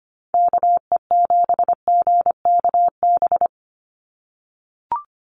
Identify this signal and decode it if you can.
Morse code, heard as a keyed tone, about 25 wpm, 700 Hz. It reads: KE7GKB
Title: USA Call Signs 25wpm